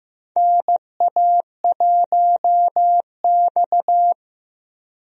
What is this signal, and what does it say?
Morse code, heard as a keyed tone, about 15 wpm, 700 Hz. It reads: NA1X
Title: USA Call Signs - 4 Characters - Repeated 2x - 15wpm